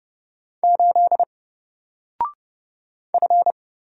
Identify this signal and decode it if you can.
Morse code, heard as a keyed tone, about 30 wpm, 700 Hz. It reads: 8 F
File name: Single-Letter-Number-30wpm